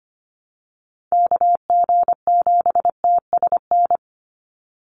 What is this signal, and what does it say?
Morse code, heard as a keyed tone, about 25 wpm, 700 Hz. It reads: KG7TSN
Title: USA Call Signs 25wpm